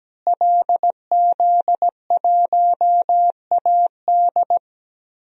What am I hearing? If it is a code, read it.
Morse code, heard as a keyed tone, about 17 wpm, 700 Hz. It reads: LZ1AD
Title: DXCC-100 Call Signs - 5 Characters - Repeated 2x - 17wpm